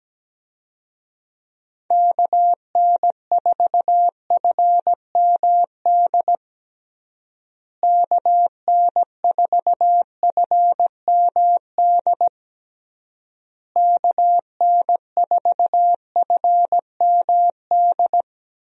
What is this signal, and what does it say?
Morse code, heard as a keyed tone, about 17 wpm, 700 Hz. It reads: KN4FMD KN4FMD KN4FMD
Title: USA Call Signs - 6 Characters - Repeated 3x - 17wpm